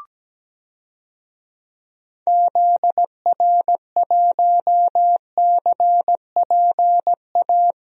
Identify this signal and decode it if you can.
Morse code, heard as a keyed tone, about 17 wpm, 700 Hz. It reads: ZR1CPA
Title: DXCC-100 Call Signs - 6 Characters - 17wpm